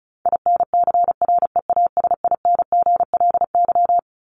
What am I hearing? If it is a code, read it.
Morse code, heard as a keyed tone, about 34 wpm, 700 Hz. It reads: INCREASINGLY
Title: LY Suffix - Top 100 Words with Base Words 35wpm